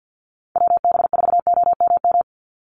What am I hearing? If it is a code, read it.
Morse code, heard as a keyed tone, about 50 wpm, 700 Hz. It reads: WB4OMM
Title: Call Signs Repeated 2x - 50wpm